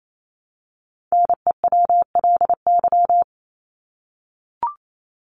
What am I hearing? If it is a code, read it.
Morse code, heard as a keyed tone, about 28 wpm, 700 Hz. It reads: NEWLY
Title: LY Suffix - Top 100 Words with Base Words 28wpm